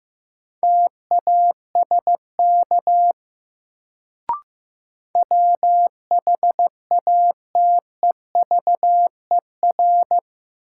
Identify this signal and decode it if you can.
Morse code, heard as a keyed tone, about 15 wpm, 700 Hz. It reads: TASK WHATEVER
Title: Sentences from Top 1100 Words - New Word Drill Alt - 15wpm